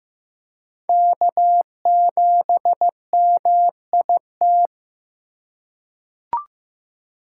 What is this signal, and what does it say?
Morse code, heard as a keyed tone, about 15 wpm, 700 Hz. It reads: K7MIT
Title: USA Call Signs - 2x Character Spacing - 15wpm